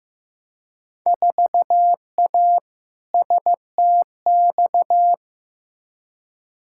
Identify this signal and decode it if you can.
Morse code, heard as a keyed tone, about 16 wpm, 700 Hz. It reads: 4A STX
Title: Field Day Exchanges Encore - 15wpm